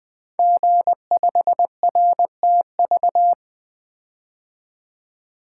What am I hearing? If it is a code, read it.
Morse code, heard as a keyed tone, about 20 wpm, 700 Hz. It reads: G5RTV
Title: Code Groups - 5 Letter:Numbers - 20wpm